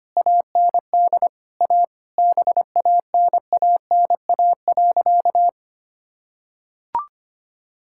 Morse code, heard as a keyed tone, about 25 wpm, 700 Hz. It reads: AND A BANANA.
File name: practice / 25wpm / Sentences with 2-Word Phrases - 25wpm